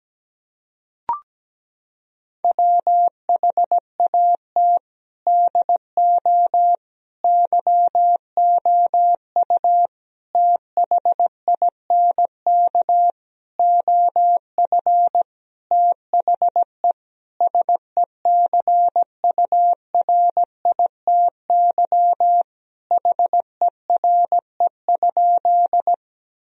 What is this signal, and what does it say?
Morse code, heard as a keyed tone, about 17 wpm, 700 Hz. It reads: WHAT DO YOU THINK OF THE SECURITY HERE?
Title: Sentences from Top 1000 Words - Review Part 4 - 17wpm